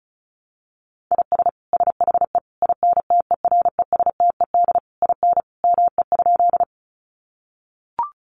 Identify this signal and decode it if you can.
Morse code, heard as a keyed tone, about 34 wpm, 700 Hz. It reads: IS SHE INTERESTED IN ME?
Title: Sentences from Top 900 Words 35wpm